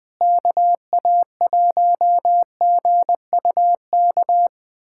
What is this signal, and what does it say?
Morse code, heard as a keyed tone, about 21 wpm, 700 Hz. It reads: KA1GUK
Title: USA Callsign DE Callsign - 6 Characters - 20wpm